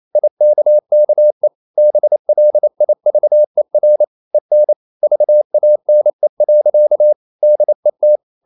Morse code, heard as a keyed tone, about 28 wpm, 600 Hz. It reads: IKKE BLIVER EN VANE. DET
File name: Ferie QRQ 28wpm